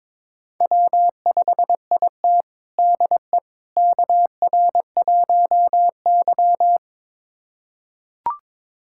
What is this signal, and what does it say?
Morse code, heard as a keyed tone, about 22 wpm, 700 Hz. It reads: W5IT DE KR1Y
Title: USA Callsign DE Callsign - 4 Characters - 22wpm